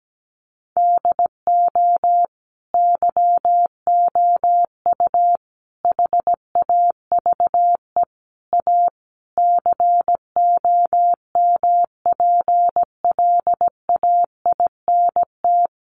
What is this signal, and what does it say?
Morse code, heard as a keyed tone, about 17 wpm, 700 Hz. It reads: DO YOU HAVE A COMPLAINT
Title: Sentences from Top 1400 Words 17wpm